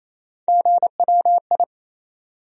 Morse code, heard as a keyed tone, about 28 wpm, 700 Hz. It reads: GWI
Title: USA Call Sign Postfixes - 3 Letters 28wpm